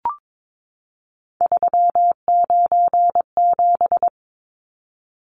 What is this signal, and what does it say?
Morse code, heard as a keyed tone, about 22 wpm, 700 Hz. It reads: 397
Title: Numbers - 3-Digit 22wpm